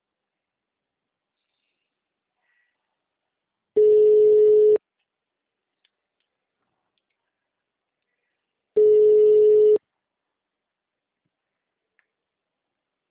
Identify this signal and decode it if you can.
Ringback tone (Europe, 425 Hz).
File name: out- -2013- - - .89